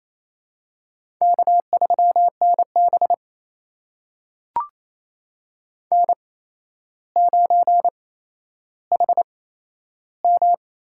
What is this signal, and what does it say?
Morse code, heard as a keyed tone, about 28 wpm, spaced as for 12 wpm, 700 Hz. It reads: K3NB N9HM
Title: USA Call Signs - 4 Characters - 8x Character Spacing - 28wpm